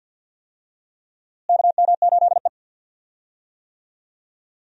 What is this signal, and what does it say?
Morse code, heard as a keyed tone, about 50 wpm, 700 Hz. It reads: KM8E
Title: USA Call Signs - 4 Characters - Repeated 2x - 50wpm